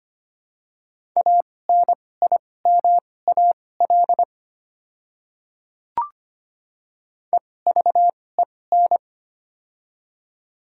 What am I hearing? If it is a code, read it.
Morse code, heard as a keyed tone, about 25 wpm, spaced as for 18 wpm, 700 Hz. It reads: ANIMAL EVEN
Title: 2x Character Spacing - Top 500 Words - 25wpm